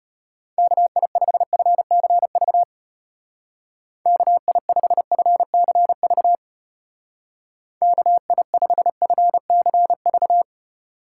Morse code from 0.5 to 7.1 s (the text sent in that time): KI5FCV KI5FCV